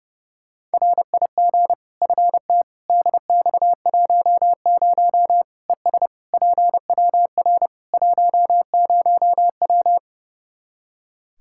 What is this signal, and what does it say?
Morse code, heard as a keyed tone, about 32 wpm, 700 Hz. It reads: RIG FT DX10 ES PWR 10W
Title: Practice QSO - Exchange 2 - Info - 30wpm